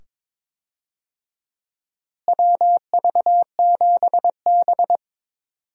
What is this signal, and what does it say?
Morse code, heard as a keyed tone, about 22 wpm, 700 Hz. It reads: WV7B
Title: USA Call Signs - 4 Characters - Repeated 2x - 22wpm